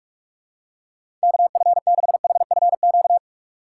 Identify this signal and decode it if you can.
Morse code, heard as a keyed tone, about 46 wpm, 700 Hz. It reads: KF6HFQ
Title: USA Call Signs - 6 Characters - 4x Character Spacing - 45wpm